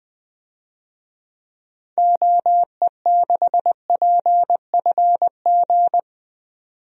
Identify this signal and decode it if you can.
Morse code, heard as a keyed tone, about 20 wpm, 700 Hz. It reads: OE6PFG
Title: DXCC-100 Call Signs - 6 Characters - 20wpm